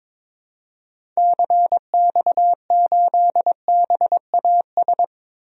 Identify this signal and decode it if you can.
Morse code, heard as a keyed tone, about 21 wpm, 700 Hz. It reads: CX8BAS
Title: DXCC-100 Call Signs - 6 Characters - 22wpm